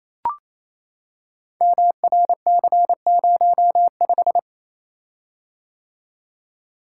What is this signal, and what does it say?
Morse code, heard as a keyed tone, about 27 wpm, 700 Hz. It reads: MRC05
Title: Code Groups - 5 Letter:Numbers - 28wpm